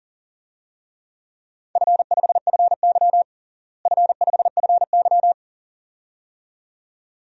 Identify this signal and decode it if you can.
Morse code, heard as a keyed tone, about 40 wpm, 700 Hz. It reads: F5FY F5FY
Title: DXCC-100 Call Signs - 4 Characters - Repeated 2x - 40wpm